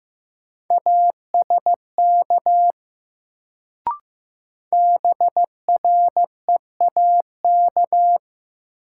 Morse code, heard as a keyed tone, about 15 wpm, 700 Hz. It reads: ASK BREAK
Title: Sets of 4 Words - New Word Drill - 15wpm